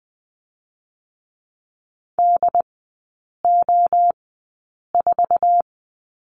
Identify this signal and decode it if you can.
Morse code, heard as a keyed tone, about 20 wpm, spaced as for 8 wpm, 690 Hz. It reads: DO4
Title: 081 Lesson 14e - Number 2 - Three Characters 20wpm